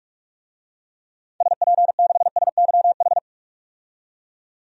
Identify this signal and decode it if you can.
Morse code, heard as a keyed tone, about 46 wpm, 700 Hz. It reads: SP6SYH
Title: DXCC-100 Call Signs - 6 Characters - 45wpm